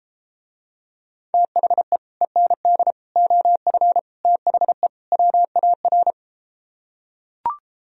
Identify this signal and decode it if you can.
Morse code, heard as a keyed tone, about 33 wpm, 700 Hz. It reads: THE END OF THE WAR